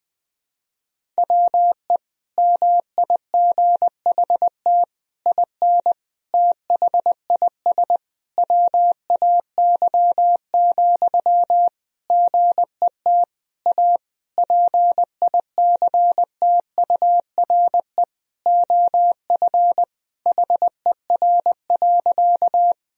Morse code, heard as a keyed tone, about 20 wpm, 700 Hz. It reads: WE MIGHT IN THIS WAY, GET A PICTURE OF HER.
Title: Sentences from Top 1000 Words - Review Part 1 - 20wpm